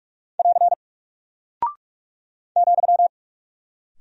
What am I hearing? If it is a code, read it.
Morse code, heard as a keyed tone, about 44 wpm, 700 Hz. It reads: <AR> ,
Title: Single Character (Letter:Number:Punctuation:Prosign) Alt 45wpm